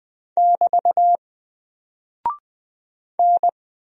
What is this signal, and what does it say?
Morse code, heard as a keyed tone, about 20 wpm, 700 Hz. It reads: <BT> N
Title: Single Character (Letter:Number:Punctuation:Prosign) 20wpm